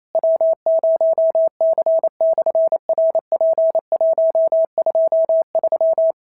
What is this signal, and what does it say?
Morse code, heard as a keyed tone, about 28 wpm, 650 Hz. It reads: W0C/RP123
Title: SOTA QSO Practice 28wpm